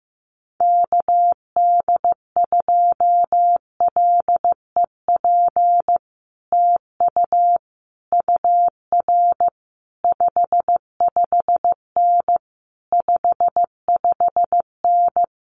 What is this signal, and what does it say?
Morse code, heard as a keyed tone, about 15 wpm, 700 Hz. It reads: KD2LEP TU UR 55N 55N
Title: Common QSO Phrases Encore 15wpm